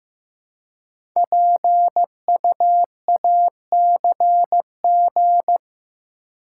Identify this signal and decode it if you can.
Morse code, heard as a keyed tone, about 15 wpm, 700 Hz. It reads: PUACG
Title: Code Groups - 5 Letters - 15wpm